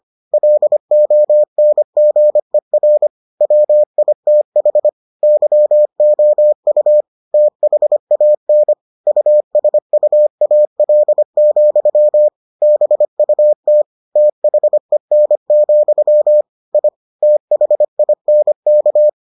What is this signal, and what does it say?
Morse code, heard as a keyed tone, about 25 wpm, 600 Hz. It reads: LONGER WITH YOU THAN USUAL, BUT THEN, I THINK